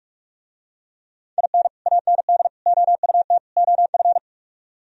Morse code, heard as a keyed tone, about 45 wpm, 700 Hz. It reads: IN AND OUT OF